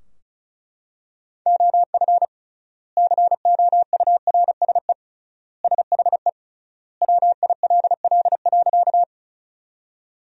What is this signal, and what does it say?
Morse code, heard as a keyed tone, about 34 wpm, 700 Hz. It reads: OF COURSE SHE WILL.
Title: 3x Word-Spacing - Sentences from Top 500 Words - 35wpm